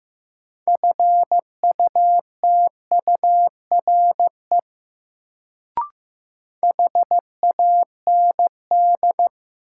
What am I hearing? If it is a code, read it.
Morse code, heard as a keyed tone, about 15 wpm, 700 Hz. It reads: FUTURE HAND
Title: Sets of 3 Words - New Word Drill - 15wpm